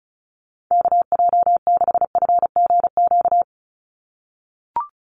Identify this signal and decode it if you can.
Morse code, heard as a keyed tone, about 34 wpm, 700 Hz. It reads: KJ6FGQ
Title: USA Call Signs 35wpm